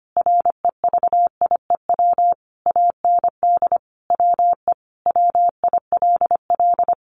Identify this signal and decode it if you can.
Morse code, heard as a keyed tone, about 25 wpm, 700 Hz. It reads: REVIEW AND WE WILL
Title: Sentences from Top 900 Words 25wpm